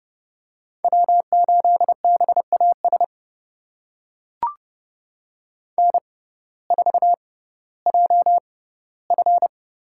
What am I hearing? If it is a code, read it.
Morse code, heard as a keyed tone, about 30 wpm, spaced as for 17 wpm, 700 Hz. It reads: W8BAS N4JF
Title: USA Call Signs - 5 Characters - 6x Character Spacing - 30wpm